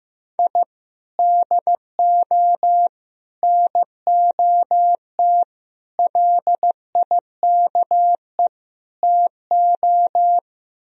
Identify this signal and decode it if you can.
Morse code, heard as a keyed tone, about 15 wpm, 700 Hz. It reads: I DO NOT LIKE TO
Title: Speed-Racing - Sets of 5 Words 15wpm